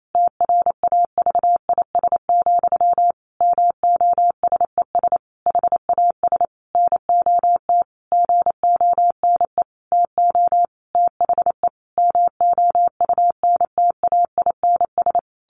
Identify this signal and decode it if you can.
Morse code, heard as a keyed tone, about 28 wpm, 700 Hz. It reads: TRAVIS, MOSES HAS NOT GONE TO THE MOUNTAINS
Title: A Sound Of Thunder 28wpm